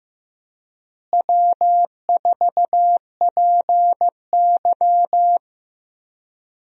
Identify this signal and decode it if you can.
Morse code, heard as a keyed tone, about 15 wpm, 700 Hz. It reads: W4PY
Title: USA Call Signs - 4 Characters - Repeated 3x - 15wpm